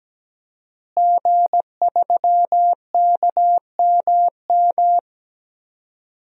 Morse code, heard as a keyed tone, about 17 wpm, 700 Hz. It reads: G3KMM
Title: DXCC-100 Call Signs - 5 Characters - Repeated 3x - 17wpm